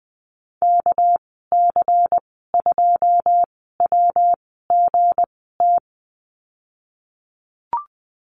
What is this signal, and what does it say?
Morse code, heard as a keyed tone, about 20 wpm, spaced as for 15 wpm, 700 Hz. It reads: KC2WGT
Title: 200 Lesson 32g - Letter k - Call Signs 20wpm